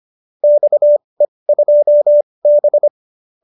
Morse code, heard as a keyed tone, about 25 wpm, spaced as for 20 wpm, 590 Hz. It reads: XE2B